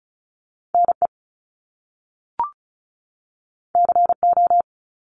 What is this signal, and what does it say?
Morse code, heard as a keyed tone, about 36 wpm, 700 Hz. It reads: NE CO
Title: Top Two Letter Combinations in Words 35wpm